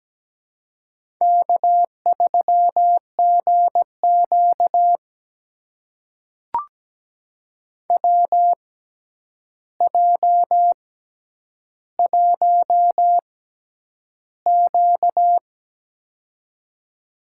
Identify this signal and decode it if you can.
Morse code, heard as a keyed tone, about 17 wpm, spaced as for 9 wpm, 700 Hz. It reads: K3GQ WJ1Q
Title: USA Call Signs - 4 Characters - 6x Character Spacing - 17wpm